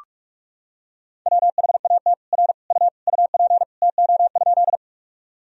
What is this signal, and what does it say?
Morse code, heard as a keyed tone, about 46 wpm, 700 Hz. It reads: WHAT R U UP TO?
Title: Sentences from Top 100 Words and Abbreviations 45wpm